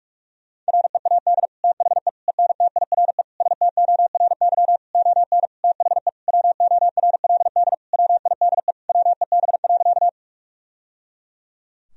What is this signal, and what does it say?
Morse code, heard as a keyed tone, about 47 wpm, 700 Hz. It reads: READ THE ENTIRE STORY ON THE WORLD WIDE WEB.
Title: Sentences from Top 1000 Words - Review Part 4 - 45wpm